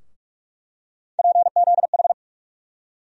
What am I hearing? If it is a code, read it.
Morse code, heard as a keyed tone, about 44 wpm, 700 Hz. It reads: PZH
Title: USA Call Sign Postfixes - 3 Letters 45wpm